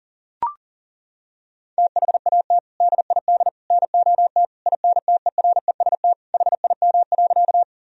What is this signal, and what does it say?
Morse code, heard as a keyed tone, about 37 wpm, 700 Hz. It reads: THAT DID NOT INTEREST HIM.
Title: Sentences from Top 1000 Words - Review Part 2 - 40wpm